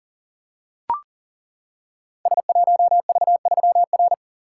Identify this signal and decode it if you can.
Morse code, heard as a keyed tone, about 38 wpm, 700 Hz. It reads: S1V3R
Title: Code Groups - 5 Letter:Numbers - 40wpm